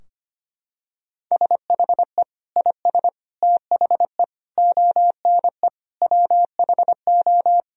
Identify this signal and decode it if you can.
Morse code, heard as a keyed tone, about 25 wpm, 700 Hz. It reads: SHE IS THE ONE WHO
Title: Sentences from Top 1000 Words - Review Part 2 - 25wpm